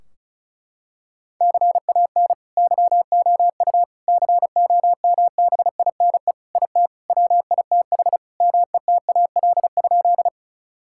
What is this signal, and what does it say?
Morse code, heard as a keyed tone, about 36 wpm, 700 Hz. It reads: CAN YOU COMBINE IT WITH METAL?
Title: Sentences from Top 1600 Words 35wpm